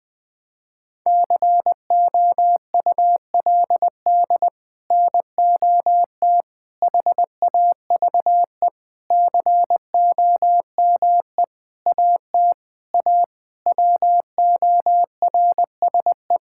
Morse code, heard as a keyed tone, about 20 wpm, 700 Hz. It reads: COULD NOT HAVE COME AT A WORSE
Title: Sets of 8 Words - 20wpm